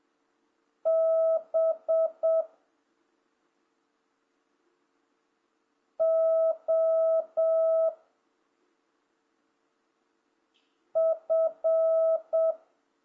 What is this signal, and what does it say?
Morse code, heard as a keyed tone, about 7 wpm, 650 Hz. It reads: B O F